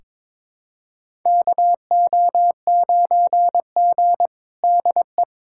Morse code, heard as a keyed tone, about 22 wpm, 700 Hz. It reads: KO9G DE
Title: USA Callsign DE Callsign - 4 Characters - 22wpm